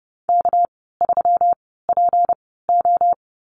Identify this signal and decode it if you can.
Morse code, heard as a keyed tone, about 30 wpm, spaced as for 17 wpm, 700 Hz. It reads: K3PO
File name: Lesson 37g - Letter x - Call Signs 30wpm